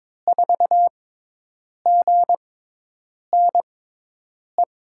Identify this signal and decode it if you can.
Morse code, heard as a keyed tone, about 22 wpm, spaced as for 8 wpm, 700 Hz. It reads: 4GNE
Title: USA Call Signs - 6x Character Spacing - 22wpm